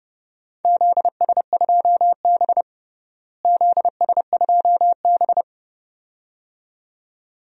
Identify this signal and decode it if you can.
Morse code, heard as a keyed tone, about 30 wpm, 700 Hz. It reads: ZS2B ZS2B
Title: DXCC-100 Call Signs - 4 Characters - Repeated 2x - 30wpm